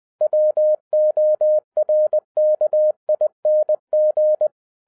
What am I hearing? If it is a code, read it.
Morse code, heard as a keyed tone, about 20 wpm, 610 Hz. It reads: WORKING